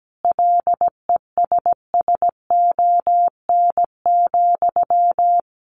Morse code, heard as a keyed tone, about 17 wpm, 700 Hz. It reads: LESSON,
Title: Sentences from Top 1200 Words 17wpm